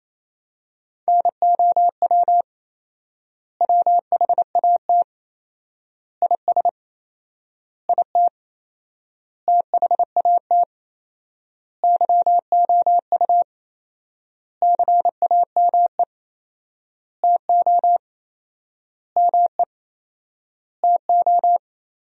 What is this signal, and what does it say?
Morse code, heard as a keyed tone, about 28 wpm, 700 Hz. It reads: NOW WHAT IS IT THAT YOU CAME TO ME TO
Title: 4x Word-Spacing - Sentences from Top 200 Words - 28wpm